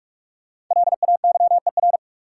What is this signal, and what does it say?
Morse code, heard as a keyed tone, about 43 wpm, 700 Hz. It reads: LAYER
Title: Sentences from Top 1600 Words - New Word Drill - 45wpm